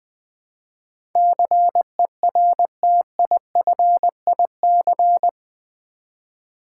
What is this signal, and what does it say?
Morse code, heard as a keyed tone, about 20 wpm, 700 Hz. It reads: CERTIFIC
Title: Build A Word - 4 Letters At A Time - 11-Letter Words - 20wpm